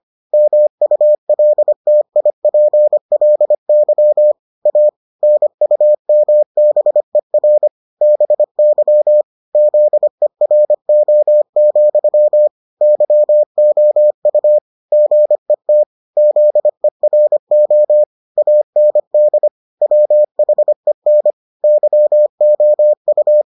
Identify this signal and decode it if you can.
Morse code, heard as a keyed tone, about 25 wpm, 600 Hz. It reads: MULTIPLY A NUMBER BY ZERO, YOU GET ZERO AND WHEN YOU